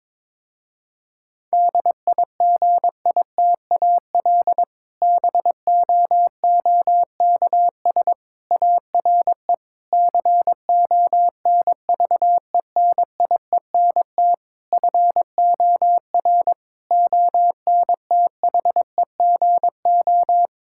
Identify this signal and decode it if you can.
Morse code, heard as a keyed tone, about 22 wpm, 700 Hz. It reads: DIGITAL BOOKS ARE CONVENIENT FOR ONTHEGO
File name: Difficult Sentences with Hints - 22wpm